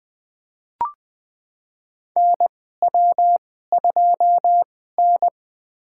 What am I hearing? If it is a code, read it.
Morse code, heard as a keyed tone, about 20 wpm, spaced as for 14 wpm, 700 Hz. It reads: NW2N
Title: USA Call Signs - 2x Character Spacing - 20wpm